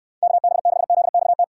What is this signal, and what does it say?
Morse code, heard as a keyed tone, about 67 wpm, 700 Hz. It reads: KB6ZXT